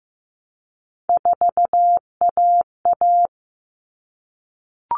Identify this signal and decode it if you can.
Morse code, heard as a keyed tone, about 15 wpm, 700 Hz. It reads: 4AA
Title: All Cut Numbers - 3-Digit 15wpm